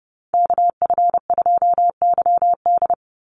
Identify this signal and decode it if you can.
Morse code, heard as a keyed tone, about 30 wpm, 700 Hz. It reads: KF2YD
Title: USA Call Signs 30wpm